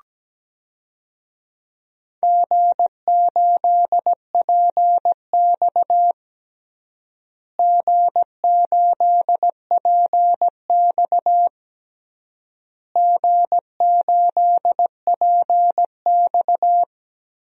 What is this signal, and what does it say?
Morse code, heard as a keyed tone, about 17 wpm, 700 Hz. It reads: G8PX G8PX G8PX
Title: DXCC-100 Call Signs - 4 Characters - Repeated 3x - 17wpm